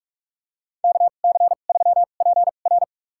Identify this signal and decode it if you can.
Morse code, heard as a keyed tone, about 44 wpm, spaced as for 32 wpm, 700 Hz. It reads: KC3PR
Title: USA Call Signs - 2x Character Spacing - 45wpm